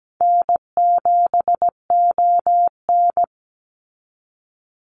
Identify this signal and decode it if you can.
Morse code, heard as a keyed tone, about 17 wpm, 700 Hz. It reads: N7ON
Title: Call Signs Repeated 3x - 17wpm